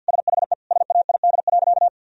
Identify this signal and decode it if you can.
Morse code, heard as a keyed tone, about 51 wpm, 700 Hz. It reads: SHE SAID.
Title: Sentences from Top 1000 Words - Review Part 1 - 50wpm